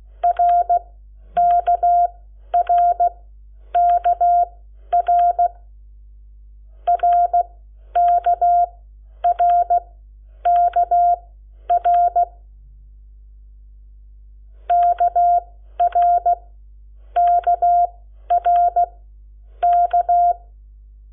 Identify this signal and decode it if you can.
Morse code, heard as a keyed tone, about 16 wpm, spaced as for 10 wpm, 700 Hz. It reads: RKRKR RKRKR KRKRK
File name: Lecciones / CLASE Nº7